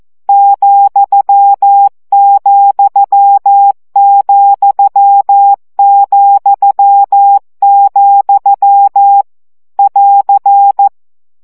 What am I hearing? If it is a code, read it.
Morse code, heard as a keyed tone, about 14 wpm, 800 Hz. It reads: ,,,,, <AR>
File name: CW gyakorlas vesszo 12 baud